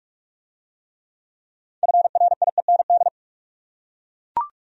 Morse code, heard as a keyed tone, about 44 wpm, 700 Hz. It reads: FRIEND